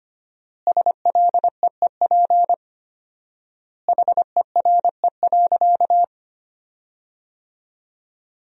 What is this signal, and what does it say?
Morse code, heard as a keyed tone, about 25 wpm, 700 Hz. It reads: SLEEP HERE.
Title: 4x Word-Spacing - Sentences from Top 500 Words - 25wpm